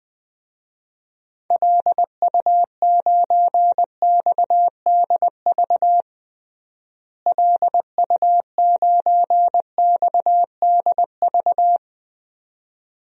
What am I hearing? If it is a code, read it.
Morse code, heard as a keyed tone, about 21 wpm, 700 Hz. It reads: LU9XDV LU9XDV
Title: DXCC-100 Call Signs - 6 Characters - Repeated 3x - 20wpm